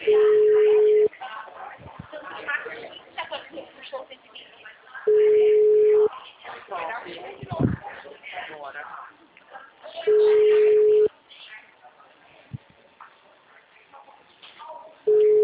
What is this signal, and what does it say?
Ringback tone (Europe, 425 Hz), with speech.
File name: out- -2011- - - .73478